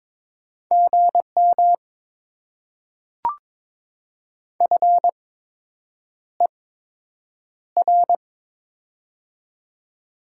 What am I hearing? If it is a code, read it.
Morse code, heard as a keyed tone, about 22 wpm, spaced as for 8 wpm, 700 Hz. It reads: GM FER
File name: Top 20 QSO Elements - 8x Character Spacing - 22wpm